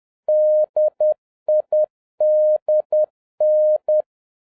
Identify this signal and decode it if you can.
Morse code, heard as a keyed tone, about 10 wpm, 610 Hz. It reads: DIDN
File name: 2011-07-mp3 / n56 BACK FROM A BIG ACCIDENT 10wpm